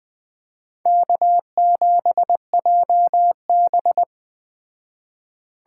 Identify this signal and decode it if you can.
Morse code, heard as a keyed tone, about 20 wpm, 700 Hz. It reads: K7JB
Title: Speed Racing - USA 1x2 Call Signs 17wpm